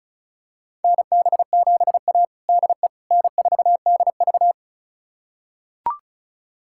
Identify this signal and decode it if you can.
Morse code, heard as a keyed tone, about 36 wpm, 700 Hz. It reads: NB7A DE N4DV
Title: USA Callsign DE Callsign - 4 Characters - 35wpm